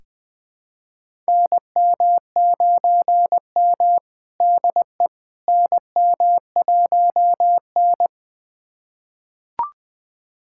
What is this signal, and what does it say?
Morse code, heard as a keyed tone, about 20 wpm, 700 Hz. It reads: NM9M DE NM1N
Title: USA Callsign DE Callsign - 4 Characters - 20wpm